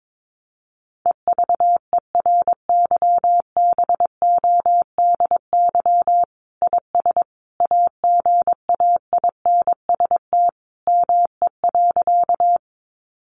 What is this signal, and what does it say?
Morse code, heard as a keyed tone, about 22 wpm, 700 Hz. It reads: EVERYBODY IS AGAINST ME.
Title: Sentences from Top 1400 Words 22wpm